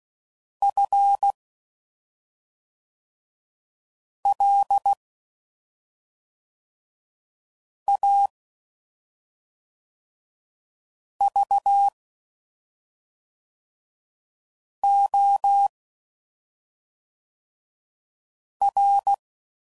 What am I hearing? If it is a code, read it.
Morse code, heard as a keyed tone, about 16 wpm, 800 Hz. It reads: F L A V O R
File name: Suffixes / Special ING's / SlowToFaster ING's